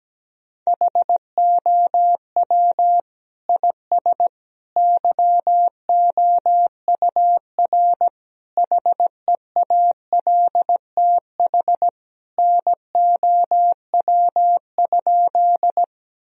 Morse code, heard as a keyed tone, about 17 wpm, 700 Hz. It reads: HOW IS YOUR HEALTH NOW?
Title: Sentences from Top 1000 Words - Review Part 3 - 17wpm